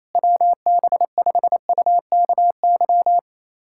Morse code, heard as a keyed tone, about 28 wpm, 700 Hz. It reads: WB5UKY
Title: USA Callsign DE Callsign - 6 Characters - 28wpm